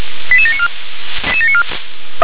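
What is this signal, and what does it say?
Five-tone selective call (ZVEI), heard as quick runs of five tones, tones of about 70 ms. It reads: 88794
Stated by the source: ZVEI five-tone call received by SDR.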